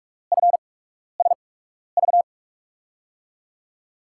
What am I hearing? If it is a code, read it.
Morse code, heard as a keyed tone, about 47 wpm, spaced as for 12 wpm, 700 Hz. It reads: FSV
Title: USA Call Signs - 5 Characters - 8x Character Spacing - 45wpm